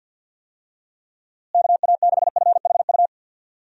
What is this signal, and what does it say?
Morse code, heard as a keyed tone, about 50 wpm, 700 Hz. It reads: KA6FHU
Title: USA Call Signs - 6 Characters - Repeated 3x - 50wpm